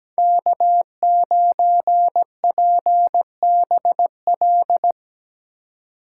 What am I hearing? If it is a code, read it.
Morse code, heard as a keyed tone, about 20 wpm, 700 Hz. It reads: K9PBL
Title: Speed Racing - USA 1x3 Call Signs 17wpm